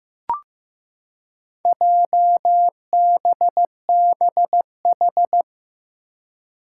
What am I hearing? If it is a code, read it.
Morse code, heard as a keyed tone, about 15 wpm, 700 Hz. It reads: JBBH
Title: Four Challenging Letter-Number Combinations 15wpm